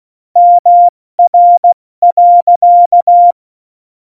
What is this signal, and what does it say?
Morse code, heard as a keyed tone, about 16 wpm, 700 Hz. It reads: MR.
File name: Morsekurs / Witze / Witz 19 Mai, 2020 70 F80